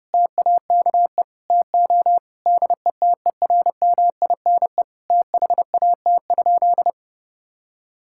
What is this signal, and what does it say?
Morse code, heard as a keyed tone, about 30 wpm, 700 Hz. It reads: TAKE TO DETERMINE THAT?
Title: Sentences from Top 1000 Words - Review Part 5 - 30wpm